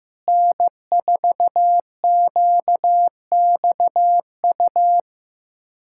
Morse code, heard as a keyed tone, about 16 wpm, 700 Hz. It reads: N4QXU
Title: USA Callsign DE Callsign - 5 Characters - 15wpm